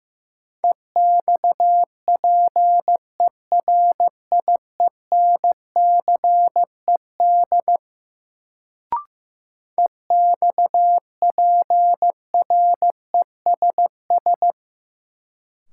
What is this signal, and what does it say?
Morse code, heard as a keyed tone, about 15 wpm, 700 Hz. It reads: EXPERIENCED EXPRESS
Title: Sentences from Top 1100 Words - New Word Drill Alt - 15wpm